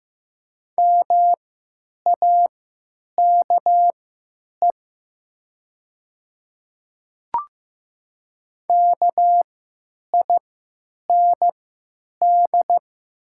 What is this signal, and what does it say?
Morse code, heard as a keyed tone, about 15 wpm, spaced as for 9 wpm, 700 Hz. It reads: MAKE KIND
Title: 3x Character Spacing - Top 300 Words - 15wpm